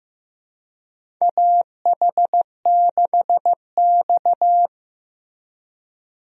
Morse code, heard as a keyed tone, about 15 wpm, 700 Hz. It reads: AH6X
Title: USA Call Signs - 4 Characters - Repeated 2x - 15wpm